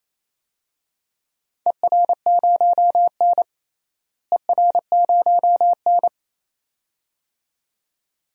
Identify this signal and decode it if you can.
Morse code, heard as a keyed tone, about 28 wpm, 700 Hz. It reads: ER0N ER0N
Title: DXCC-100 Call Signs - 4 Characters - Repeated 2x - 28wpm